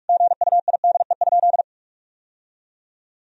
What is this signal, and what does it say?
Morse code, heard as a keyed tone, about 46 wpm, 700 Hz. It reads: GUIDE?
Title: Sentences from Top 1000 Words - Review Part 3 - 45wpm